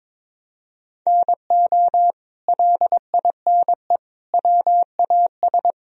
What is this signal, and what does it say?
Morse code, heard as a keyed tone, about 21 wpm, 700 Hz. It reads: NO LINE WAS
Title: Sentences from Top 1000 Words - Review Part 1 - 22wpm